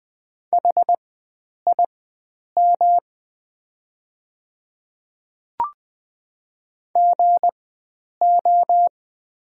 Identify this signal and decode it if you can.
Morse code, heard as a keyed tone, about 20 wpm, spaced as for 9 wpm, 690 Hz. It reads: HIM GO
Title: 4x Character Spacing - Top 100 Words - 20wpm